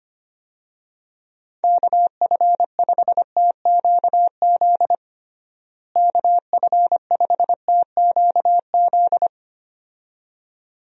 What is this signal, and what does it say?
Morse code, heard as a keyed tone, about 25 wpm, 700 Hz. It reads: KF5TQZ KF5TQZ
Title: USA Call Signs - 6 Characters - Repeated 2x - 25wpm